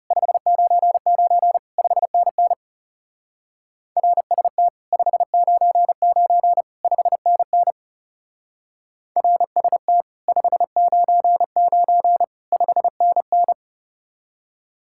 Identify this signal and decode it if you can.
Morse code, heard as a keyed tone, about 36 wpm, 700 Hz. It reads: 599 5NN RST 599 5NN RST 599 5NN
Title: Speed-Racing Common QSO Phrases Encore - 30wpm